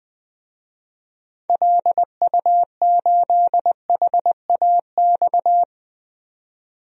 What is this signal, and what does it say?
Morse code, heard as a keyed tone, about 20 wpm, 700 Hz. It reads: LU8HAX
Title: DXCC-100 Call Signs - 6 Characters - 20wpm